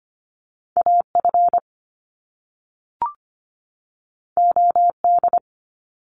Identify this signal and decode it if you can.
Morse code, heard as a keyed tone, about 25 wpm, 700 Hz. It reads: AF OD
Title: Top Two Letter Combinations in Words 25wpm